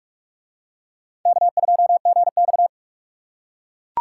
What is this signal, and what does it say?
Morse code, heard as a keyed tone, about 42 wpm, 700 Hz. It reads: K2GX